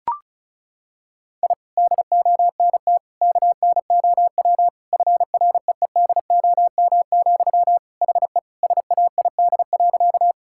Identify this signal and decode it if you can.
Morse code, heard as a keyed tone, about 36 wpm, 700 Hz. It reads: I DONT KNOW FREEDOM, HE SAID.
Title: Sentences from Top 1000 Words - Review Part 5 - 35wpm